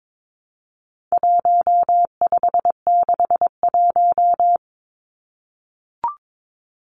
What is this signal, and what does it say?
Morse code, heard as a keyed tone, about 22 wpm, 700 Hz. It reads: 1561
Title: Numbers - 4-Digit 22wpm